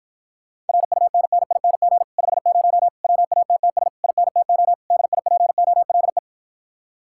Morse code, heard as a keyed tone, about 54 wpm, 700 Hz. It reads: RUNNING 50 WATTS INTO DIPOLE